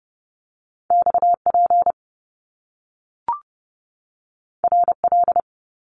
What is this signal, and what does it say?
Morse code, heard as a keyed tone, about 30 wpm, 700 Hz. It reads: XP RL
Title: Top Two Letter Combinations in Words 30wpm